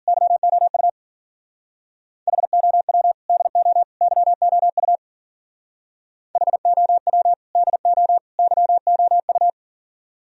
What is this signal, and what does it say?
Morse code, heard as a keyed tone, about 48 wpm, 700 Hz. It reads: YOU HOW DO YOU HOW DO YOU
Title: Speed-Racing - Sets of 3 Words 40wpm